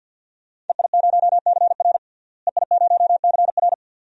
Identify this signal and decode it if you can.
Morse code, heard as a keyed tone, about 52 wpm, 700 Hz. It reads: EI0CR EI0CR
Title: DXCC-100 Call Signs - 5 Characters - Repeated 2x - 50wpm